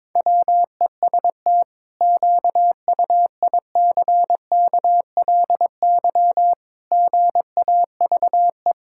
Morse code, heard as a keyed tone, about 22 wpm, 700 Hz. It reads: WEST QUICKLY GAVE
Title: Pangram Challenge v2 - 22wpm